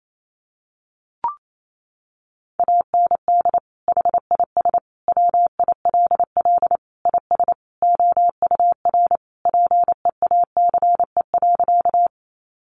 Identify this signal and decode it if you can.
Morse code, heard as a keyed tone, about 28 wpm, 700 Hz. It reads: AND HIS WILL IS OUR PEACE.
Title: Sentences from Top 1500 Words 28wpm